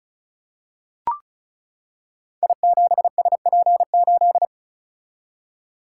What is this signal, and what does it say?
Morse code, heard as a keyed tone, about 36 wpm, 690 Hz. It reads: I7SP8
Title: Code Groups - 5 Letter:Numbers - 35wpm